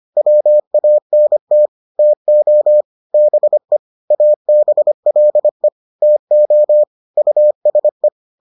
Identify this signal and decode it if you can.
Morse code, heard as a keyed tone, about 25 wpm, 600 Hz. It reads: WANT TO BE ABLE TO USE